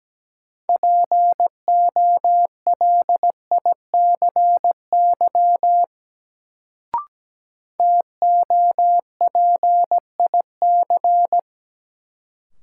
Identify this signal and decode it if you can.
Morse code, heard as a keyed tone, about 17 wpm, 700 Hz. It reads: POLICY TOPIC
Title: Sentences from Top 1200 Words - New Word Drill Alt - 17wpm